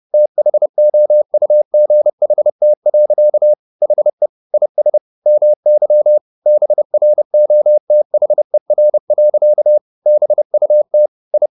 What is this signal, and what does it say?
Morse code, heard as a keyed tone, about 30 wpm, 600 Hz. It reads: THOUGHT. HE IS MY BROTHER. BUT I
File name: the old man and the sea / the old man and the sea 1 30wpm